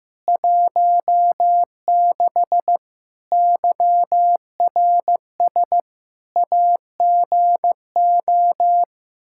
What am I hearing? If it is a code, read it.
Morse code, heard as a keyed tone, about 15 wpm, 700 Hz. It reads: 16 YRS AGO
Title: Practice QSO - Exchange 3 - More Info - 15wpm